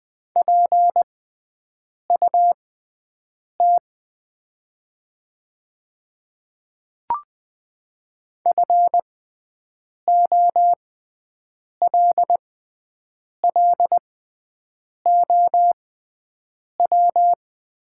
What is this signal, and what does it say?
Morse code, heard as a keyed tone, about 20 wpm, spaced as for 7 wpm, 700 Hz. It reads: PUT FOLLOW
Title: 6x Character Spacing - Top 300 Words - 20wpm